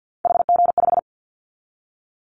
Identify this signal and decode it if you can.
Morse code, heard as a keyed tone, about 50 wpm, 700 Hz. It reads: HG5
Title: Callsign Prefixes for Top 50 DXCC - 50wpm